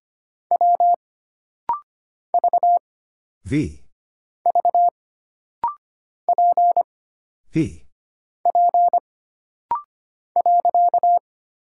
Morse code, heard as a keyed tone, about 25 wpm, 700 Hz. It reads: W V V P P .
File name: Single Character (Letter:Number:Punctuation:Prosign) - Rapid-Fire - 25wpm